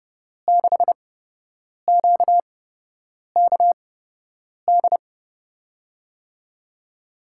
Morse code, heard as a keyed tone, about 30 wpm, spaced as for 8 wpm, 700 Hz. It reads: 6QKD
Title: USA Call Signs - 6 Characters - 8x Character Spacing - 30wpm